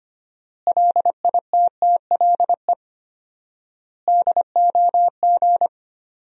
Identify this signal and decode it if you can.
Morse code, heard as a keyed tone, about 25 wpm, 700 Hz. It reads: LITTLE DOG
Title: 4x Word-Spacing - Sentences from Top 500 Words - 25wpm